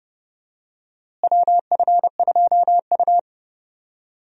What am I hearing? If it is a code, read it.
Morse code, heard as a keyed tone, about 31 wpm, 700 Hz. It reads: WF2U
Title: USA Call Signs - 4 Characters - Repeated 3x - 30wpm